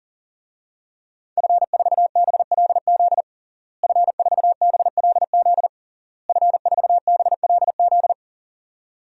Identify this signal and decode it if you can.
Morse code, heard as a keyed tone, about 40 wpm, 700 Hz. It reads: F4BLZ F4BLZ F4BLZ
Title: DXCC-100 Call Signs - 5 Characters - Repeated 3x - 40wpm